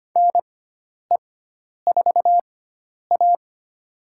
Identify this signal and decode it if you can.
Morse code, heard as a keyed tone, about 25 wpm, spaced as for 10 wpm, 700 Hz. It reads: NE4A
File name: Lesson 16g - Letter c - Call Signs 25wpm